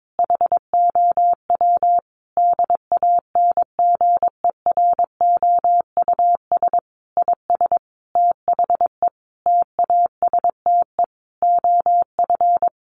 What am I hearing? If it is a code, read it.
Morse code, heard as a keyed tone, about 22 wpm, 700 Hz. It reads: HOW DANGEROUS IS THE TASTE OF
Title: Sentences from Top 1200 Words 22wpm